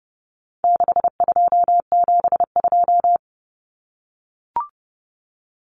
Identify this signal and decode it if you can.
Morse code, heard as a keyed tone, about 30 wpm, 700 Hz. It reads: 6272
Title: Common Cut Numbers - 4-Digit 30wpm